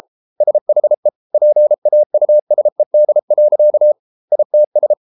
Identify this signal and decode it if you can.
Morse code, heard as a keyed tone, about 33 wpm, 600 Hz. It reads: SHE PAUSED. ITS